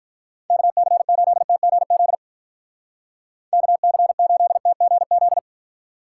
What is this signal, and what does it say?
Morse code, heard as a keyed tone, about 45 wpm, 700 Hz. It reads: KC8TGZ KC8TGZ